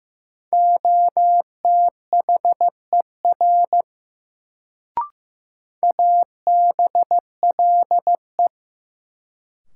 Morse code, heard as a keyed tone, about 15 wpm, 700 Hz. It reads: OTHER ABLE
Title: Sets of 2 Words - New Word Drill - 15wpm